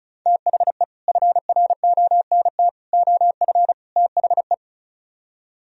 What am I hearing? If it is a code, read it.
Morse code, heard as a keyed tone, about 45 wpm, 700 Hz. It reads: THE FRONT OF THE